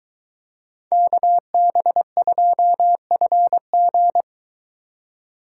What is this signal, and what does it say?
Morse code, heard as a keyed tone, about 22 wpm, 700 Hz. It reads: KB2FG